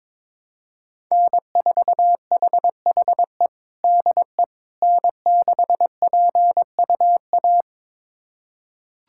Morse code, heard as a keyed tone, about 22 wpm, 700 Hz. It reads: N4HHE DE N6PUA
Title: USA Callsign DE Callsign - 5 Characters - 22wpm